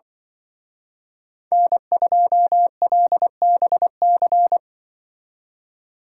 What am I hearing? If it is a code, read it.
Morse code, heard as a keyed tone, about 24 wpm, 700 Hz. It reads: N2LBC
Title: Speed Racing - USA 1x3 Call Signs 20wpm